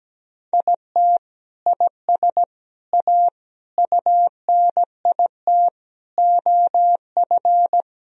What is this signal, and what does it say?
Morse code, heard as a keyed tone, about 17 wpm, 700 Hz. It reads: IT IS A UNIT OF
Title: Sentences from Top 1000 Words - Review Part 2 - 17wpm